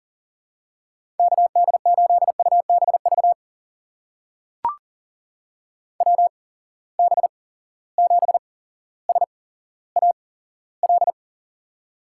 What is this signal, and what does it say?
Morse code, heard as a keyed tone, about 39 wpm, spaced as for 17 wpm, 700 Hz. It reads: KD8UBV WB7SAL
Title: USA Call Signs - 6 Characters - 8x Character Spacing - 40wpm